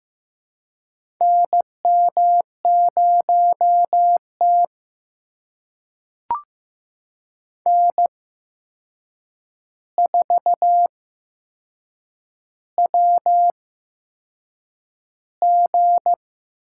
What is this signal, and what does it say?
Morse code, heard as a keyed tone, about 15 wpm, spaced as for 6 wpm, 700 Hz. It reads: NM0T N4WG
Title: USA Call Signs - 4 Characters - 8x Character Spacing - 15wpm